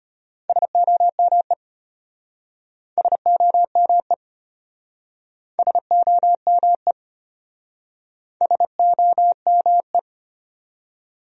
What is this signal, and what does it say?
Morse code, heard as a keyed tone, about 33 wpm, 700 Hz. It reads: SOME SOME SOME SOME